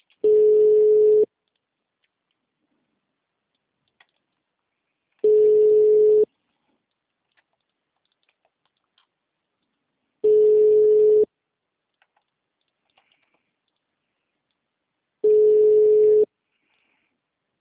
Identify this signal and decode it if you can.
Ringback tone (Europe, 425 Hz).